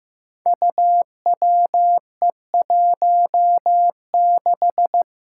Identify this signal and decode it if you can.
Morse code, heard as a keyed tone, about 15 wpm, 700 Hz. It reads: UWE16
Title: Five Challenging Letter-Number Combinations 15wpm